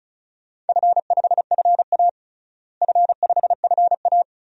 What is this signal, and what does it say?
Morse code, heard as a keyed tone, about 36 wpm, 700 Hz. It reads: F5FA F5FA
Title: DXCC-100 Call Signs - 4 Characters - Repeated 2x - 35wpm